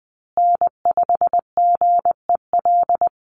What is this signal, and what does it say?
Morse code, heard as a keyed tone, about 20 wpm, 700 Hz. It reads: N5GEL
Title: USA Call Signs Repeated 2x 20wpm